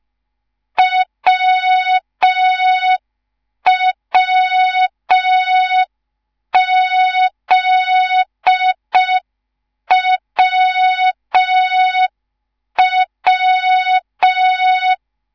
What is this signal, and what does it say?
Morse code, heard as a keyed tone, about 5 wpm, 3700 Hz. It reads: WWZWW
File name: Morze-HA2MN / hang / 10GWZ HA2MN